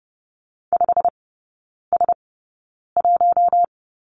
Morse code, heard as a keyed tone, about 30 wpm, spaced as for 9 wpm, 700 Hz. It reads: 5S1
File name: Lesson 16e - Letter c - Three Characters 30wpm